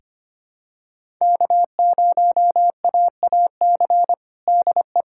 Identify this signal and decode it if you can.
Morse code, heard as a keyed tone, about 25 wpm, 700 Hz. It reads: K0AAC DE
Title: USA Callsign DE Callsign - 5 Characters - 25wpm